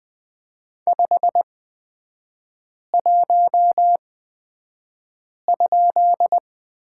Morse code, heard as a keyed tone, about 20 wpm, spaced as for 5 wpm, 700 Hz. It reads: 51?
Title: 255 Lesson 41e - Period - Three Characters 20wpm